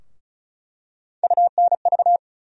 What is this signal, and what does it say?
Morse code, heard as a keyed tone, about 36 wpm, 700 Hz. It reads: UNV
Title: USA Call Sign Postfixes - 3 Letters 35wpm